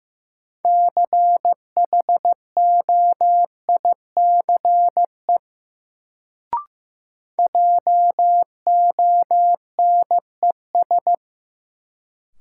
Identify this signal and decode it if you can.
Morse code, heard as a keyed tone, about 15 wpm, 700 Hz. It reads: CHOICE JONES
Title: Sets of 5 Words - New Word Drill - 15wpm